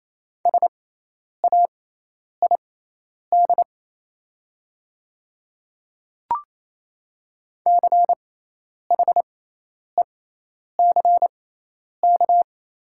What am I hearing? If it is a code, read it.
Morse code, heard as a keyed tone, about 28 wpm, spaced as for 10 wpm, 700 Hz. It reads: SAID CHECK
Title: 6x Character Spacing - Top 500 Words - 28wpm